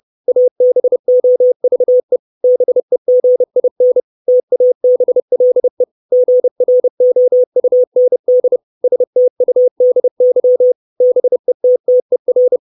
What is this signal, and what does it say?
Morse code, heard as a keyed tone, about 30 wpm, 500 Hz. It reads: ABOVE BEGIN TABLE GROUND STUDY BETTER
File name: 632 Words 30WPM